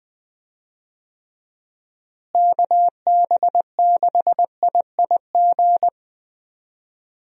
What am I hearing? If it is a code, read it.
Morse code, heard as a keyed tone, about 21 wpm, 700 Hz. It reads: KB6IIG
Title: USA Call Signs - 6 Characters - Repeated 2x - 20wpm